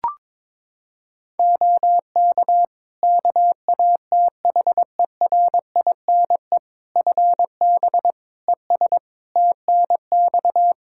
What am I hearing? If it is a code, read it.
Morse code, heard as a keyed tone, about 22 wpm, 700 Hz. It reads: OK KATHERINE FB ES TNX
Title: K1USN SST QSO Practice 22wpm